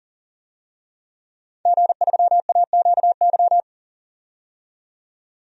Morse code, heard as a keyed tone, about 39 wpm, 700 Hz. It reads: G3AQY
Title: DXCC-100 Call Signs - 5 Characters - 40wpm